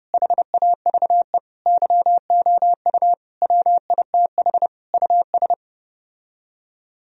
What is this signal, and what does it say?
Morse code, heard as a keyed tone, about 32 wpm, 700 Hz. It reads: HAVE YOU WITH US